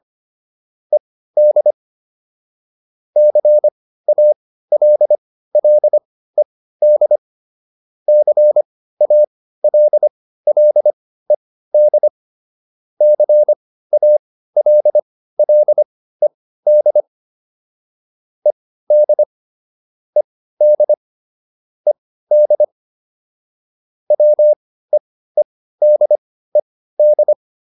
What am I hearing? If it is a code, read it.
Morse code, heard as a keyed tone, about 25 wpm, spaced as for 15 wpm, 600 Hz. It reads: ED CALLED CALLED CALLED ED ED ED WEEDED